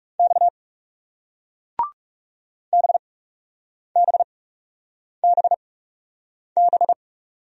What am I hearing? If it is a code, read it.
Morse code, heard as a keyed tone, about 39 wpm, spaced as for 8 wpm, 700 Hz. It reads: X BBBB